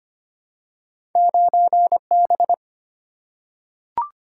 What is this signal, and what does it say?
Morse code, heard as a keyed tone, about 25 wpm, 690 Hz. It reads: 9B
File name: Two Letter-Number Combinations - 25wpm